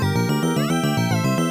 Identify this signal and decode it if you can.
There is no code.